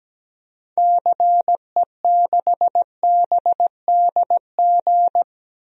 Morse code, heard as a keyed tone, about 17 wpm, 700 Hz. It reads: CE6BDG
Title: DXCC-100 Call Signs - 6 Characters - Repeated 3x - 17wpm